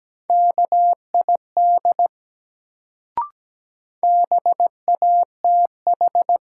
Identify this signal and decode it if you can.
Morse code, heard as a keyed tone, about 17 wpm, 700 Hz. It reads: KID BATH
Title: Sentences from Top 1600 Words - New Word Drill Alt - 17wpm